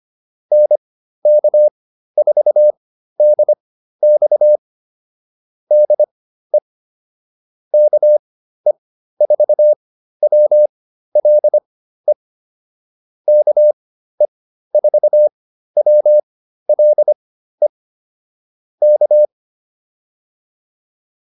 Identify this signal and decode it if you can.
Morse code, heard as a keyed tone, about 25 wpm, spaced as for 13 wpm, 600 Hz. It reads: NK4DX DE KE4WLE KE4WLE K